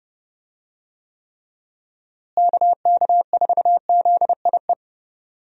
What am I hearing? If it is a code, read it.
Morse code, heard as a keyed tone, about 32 wpm, 700 Hz. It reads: KK4ZIE
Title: USA Call Signs - 6 Characters - Repeated 3x - 30wpm